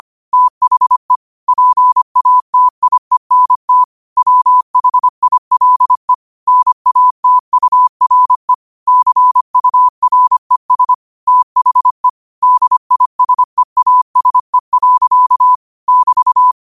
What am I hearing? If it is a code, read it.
Morse code, heard as a keyed tone, about 25 wpm, 1000 Hz. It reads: THE PATIENT WHILE NATURE CURES THE DISEASE. <BT>
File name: 25wpm